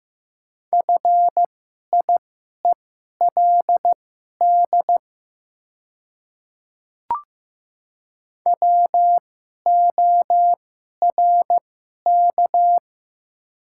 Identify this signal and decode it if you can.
Morse code, heard as a keyed tone, about 15 wpm, spaced as for 11 wpm, 700 Hz. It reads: FIELD WORK
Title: 2x Character Spacing - Top 500 Words - 15wpm